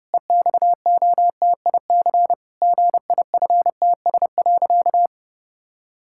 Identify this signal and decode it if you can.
Morse code, heard as a keyed tone, about 32 wpm, 700 Hz. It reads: EXOTIC GIFTS.
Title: Pangram Challenge v2 - 30wpm